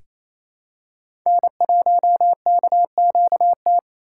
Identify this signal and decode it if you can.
Morse code, heard as a keyed tone, about 28 wpm, 700 Hz. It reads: N1KQT
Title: USA Call Signs - 5 Characters - 4x Character Spacing - 28wpm